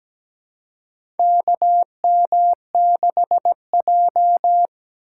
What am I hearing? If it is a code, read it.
Morse code, heard as a keyed tone, about 17 wpm, 700 Hz. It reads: KM6J
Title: USA Call Signs - 4 Characters - 3x Character Spacing - 17wpm